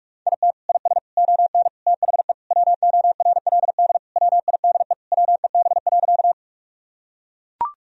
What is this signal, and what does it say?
Morse code, heard as a keyed tone, about 41 wpm, 700 Hz. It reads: IT IS ON THE WORLD WIDE WEB.
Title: Sentences from Top 1000 Words - Review Part 4 - 45wpm